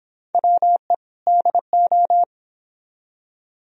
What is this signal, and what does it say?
Morse code, heard as a keyed tone, about 30 wpm, 700 Hz. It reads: WE DO